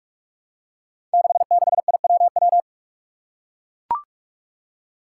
Morse code, heard as a keyed tone, about 44 wpm, 700 Hz. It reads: 66IWW
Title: Five Challenging Letter-Number Combinations 45wpm